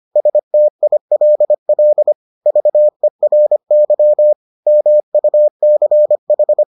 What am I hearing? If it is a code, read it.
Morse code, heard as a keyed tone, about 25 wpm, 600 Hz. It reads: STILL VERY MUCH